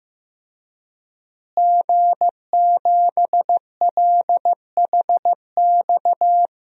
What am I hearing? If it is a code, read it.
Morse code, heard as a keyed tone, about 16 wpm, 700 Hz. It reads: G7LHX
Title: DXCC-100 Call Signs - 5 Characters - Repeated 3x - 15wpm